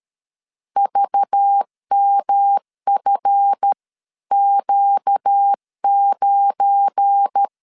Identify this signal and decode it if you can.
Morse code, heard as a keyed tone, about 13 wpm, 800 Hz. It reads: VMFQ9
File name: CW / DET / 1 9 1500C S C 12wpm 3811